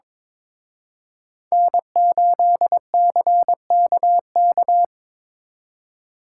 Morse code, heard as a keyed tone, about 25 wpm, 700 Hz. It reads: N8CKK
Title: Speed Racing - USA Call Signs 22wpm